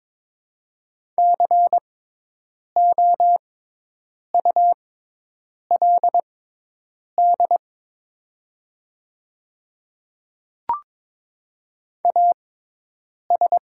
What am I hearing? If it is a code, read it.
Morse code, heard as a keyed tone, about 22 wpm, spaced as for 8 wpm, 700 Hz. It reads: COULD AS
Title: 6x Character Spacing - Top 100 Words - 22wpm